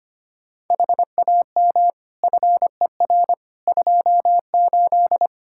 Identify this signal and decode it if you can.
Morse code, heard as a keyed tone, about 25 wpm, 700 Hz. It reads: HAM FER 28
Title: Practice QSO - Exchange 3 - More Info - 25wpm